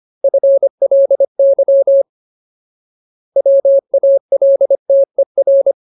Morse code, heard as a keyed tone, about 25 wpm, 550 Hz. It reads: FLY WALTER
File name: Ultimate Word List / 25WPM Ultimate